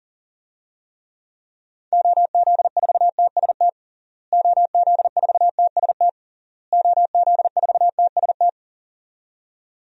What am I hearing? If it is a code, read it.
Morse code, heard as a keyed tone, about 40 wpm, 700 Hz. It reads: OZ4TST OZ4TST OZ4TST
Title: DXCC-100 Call Signs - 6 Characters - Repeated 3x - 40wpm